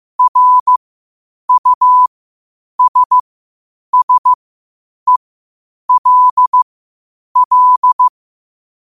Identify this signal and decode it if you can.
Morse code, heard as a keyed tone, about 15 wpm, spaced as for 8 wpm, 1000 Hz. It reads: RUSSELL
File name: qotdfiles / 8wpm